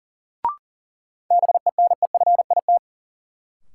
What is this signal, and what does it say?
Morse code, heard as a keyed tone, about 40 wpm, 700 Hz. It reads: BENEFIT
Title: Sentences from Top 1300 Words - New Word Drill Alt - 40wpm